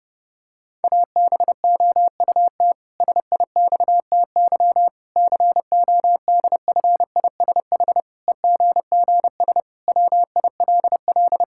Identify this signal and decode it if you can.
Morse code, heard as a keyed tone, about 30 wpm, 700 Hz. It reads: ABOUT SIXTY CODFISH EGGS WILL
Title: Pangram Challenge v2 - 30wpm